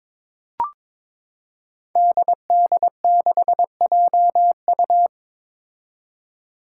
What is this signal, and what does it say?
Morse code, heard as a keyed tone, about 22 wpm, 700 Hz. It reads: DD6JU
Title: Five Challenging Letter-Number Combinations 22wpm